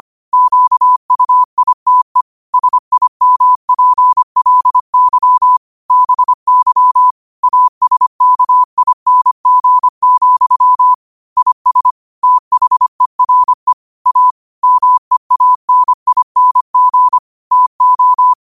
Morse code, heard as a keyed tone, about 25 wpm, 1000 Hz. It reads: QUITE SIMPLY BY ASKING, IS THERE A MEANING TO
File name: qotdfiles / 25wpm